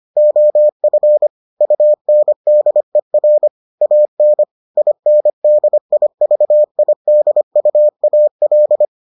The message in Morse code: OF UNDER AN INDIVIDUAL

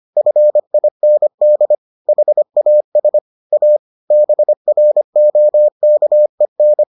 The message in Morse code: FIND HAS A BROKEN

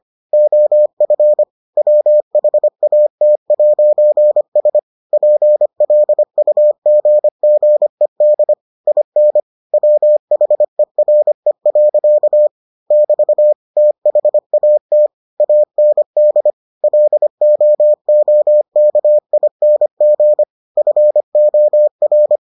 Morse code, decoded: OF WHAT'S PLUGGED IN WHERE. <BT> THAT AND LOOKING FOR

